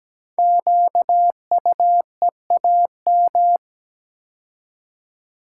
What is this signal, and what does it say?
Morse code, heard as a keyed tone, about 17 wpm, 700 Hz. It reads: QUEAM
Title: Code Groups - 5 Letters - 17wpm